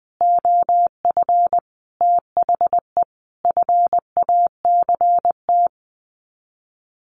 Morse code, heard as a keyed tone, about 20 wpm, 700 Hz. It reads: OF THE FACT
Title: Sets of 5 Words Encore2 20wpm